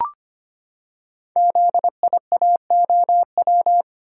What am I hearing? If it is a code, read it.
Morse code, heard as a keyed tone, about 25 wpm, 700 Hz. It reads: ZIAOW
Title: Code Groups - 5 Letter:Numbers - 25wpm